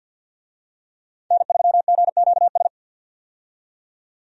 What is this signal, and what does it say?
Morse code, heard as a keyed tone, about 49 wpm, 700 Hz. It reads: N3GQS
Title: USA Call Signs - 5 Characters - 50wpm